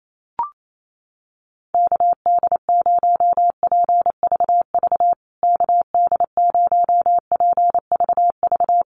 Morse code, heard as a keyed tone, about 28 wpm, 700 Hz. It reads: KD0PVV KD0PVV
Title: USA Call Signs Repeated 3x 28wpm